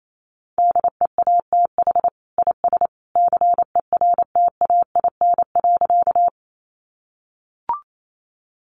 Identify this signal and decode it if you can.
Morse code, heard as a keyed tone, about 28 wpm, 700 Hz. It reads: DEATH IS CERTAIN.
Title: Sentences from Top 1100 Words 28wpm